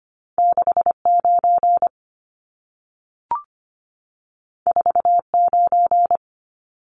Morse code, heard as a keyed tone, about 25 wpm, 700 Hz. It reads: 69 49
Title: Numbers - 2-Digit 25wpm